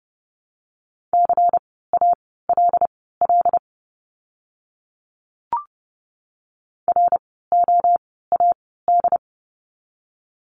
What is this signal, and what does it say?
Morse code, heard as a keyed tone, about 30 wpm, spaced as for 17 wpm, 700 Hz. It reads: CALL ROAD
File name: Lesson 16f - Letter c - Words 30wpm